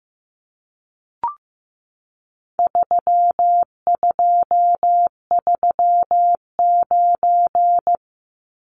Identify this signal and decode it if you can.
Morse code, heard as a keyed tone, about 15 wpm, 700 Hz. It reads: 3239
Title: Numbers - 4-Digit 15wpm